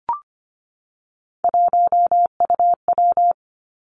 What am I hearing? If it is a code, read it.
Morse code, heard as a keyed tone, about 25 wpm, 700 Hz. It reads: 1UW
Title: Three Challenging Letter-Number Combinations 25wpm